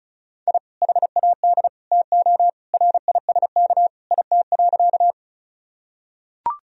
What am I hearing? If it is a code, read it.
Morse code, heard as a keyed tone, about 36 wpm, 700 Hz. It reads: I HAD TO RISK IT.
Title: Sentences from Top 1000 Words - Review Part 4 - 35wpm